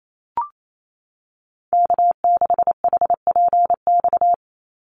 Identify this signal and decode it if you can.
Morse code, heard as a keyed tone, about 28 wpm, 700 Hz. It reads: K6HPX
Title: NA SOTA Activator Callsigns 28wpm